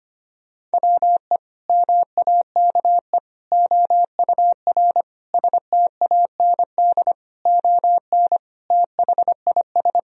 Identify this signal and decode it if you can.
Morse code, heard as a keyed tone, about 25 wpm, 700 Hz. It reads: WE MAKE OUR STAND ON THIS
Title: Sentences from Top 1000 Words - Review Part 2 - 25wpm